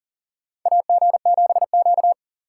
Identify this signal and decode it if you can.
Morse code, heard as a keyed tone, about 40 wpm, 700 Hz. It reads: AG7Q
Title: USA Call Signs - 4 Characters - 4x Character Spacing - 40wpm